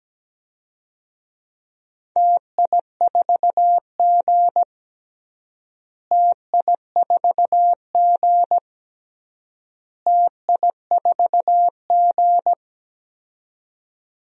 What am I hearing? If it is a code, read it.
Morse code, heard as a keyed tone, about 17 wpm, 700 Hz. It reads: TI4G TI4G TI4G
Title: DXCC-100 Call Signs - 4 Characters - Repeated 3x - 17wpm